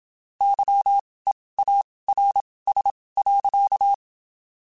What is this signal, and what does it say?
Morse code, heard as a keyed tone, about 27 wpm, spaced as for 19 wpm, 800 Hz. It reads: YEARS.
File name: (spaced) / QSOs Spaced-mp3s / kc7sac @25 sp